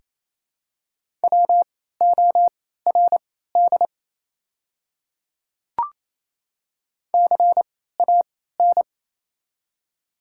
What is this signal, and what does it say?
Morse code, heard as a keyed tone, about 27 wpm, spaced as for 16 wpm, 700 Hz. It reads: WORD CAN
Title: 3x Character Spacing - Top 100 Words - 28wpm